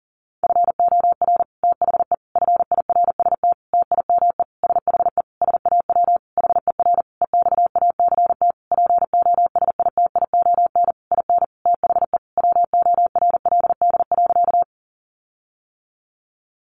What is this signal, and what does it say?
Morse code, heard as a keyed tone, about 40 wpm, 700 Hz. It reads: FOR THE FIRST TIME SHE SAW HER EXACT POSITION IN THE WORLD.
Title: Sentences from Top 1300 Words 40wpm